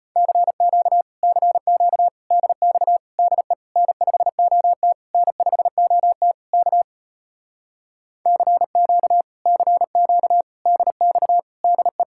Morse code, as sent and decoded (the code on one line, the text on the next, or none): -.-. --.- / -.-. --.- / -.. -..- / -.. / -. ..... --- - / -. ..... --- - / -.- / -.-. --.- / -.-. --.- / -.. -..- / -.. .
CQ CQ DX DE N5OT N5OT K CQ CQ DX DE